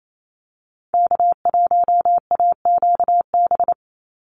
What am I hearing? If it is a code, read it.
Morse code, heard as a keyed tone, about 28 wpm, 700 Hz. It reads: K1AQB
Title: Call Signs Repeated 3x - 28wpm